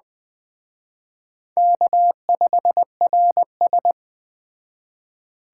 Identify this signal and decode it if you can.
Morse code, heard as a keyed tone, about 21 wpm, 700 Hz. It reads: K5RS